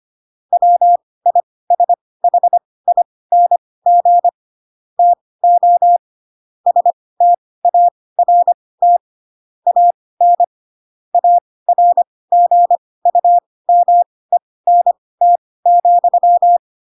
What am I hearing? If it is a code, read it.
Morse code, heard as a keyed tone, about 25 wpm, spaced as for 18 wpm, 700 Hz. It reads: WISHING TO START AN ARGUMENT,
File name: Pinocchio ch3 12 25 ik1hge